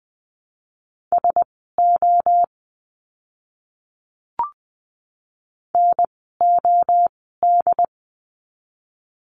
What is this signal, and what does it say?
Morse code, heard as a keyed tone, about 20 wpm, spaced as for 15 wpm, 700 Hz. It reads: SO NOD
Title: 068 Lesson 12f - Letter d - Words 20wpm